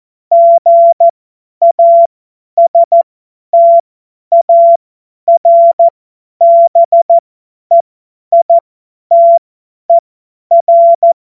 Morse code, heard as a keyed tone, about 14 wpm, spaced as for 10 wpm, 680 Hz. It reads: GASTARBEITER
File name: Morsekurs / Witze / Witz 19. April 2022 50 F70